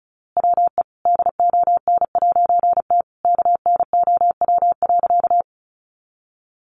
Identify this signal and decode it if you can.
Morse code, heard as a keyed tone, about 34 wpm, 700 Hz. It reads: WE DON'T KNOW.
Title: A Sound Of Thunder 35wpm